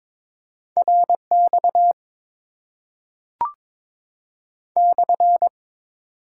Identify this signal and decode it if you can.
Morse code, heard as a keyed tone, about 22 wpm, 700 Hz. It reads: RX /
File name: Top 90 QSO Elements - 8x Character Spacing - 22wpm